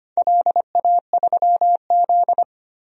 Morse code, heard as a keyed tone, about 25 wpm, 700 Hz. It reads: LA3Z
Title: DXCC-100 Call Signs - 4 Characters - 25wpm